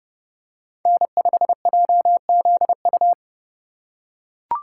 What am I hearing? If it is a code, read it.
Morse code, heard as a keyed tone, about 29 wpm, 700 Hz. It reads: N5JZU